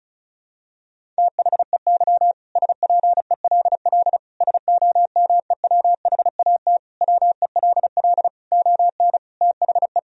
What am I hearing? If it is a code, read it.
Morse code, heard as a keyed tone, about 36 wpm, 700 Hz. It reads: THEY SPELL SOMEWHAT WELL ON THE
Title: Sentences from Top 1000 Words - Review Part 1 - 35wpm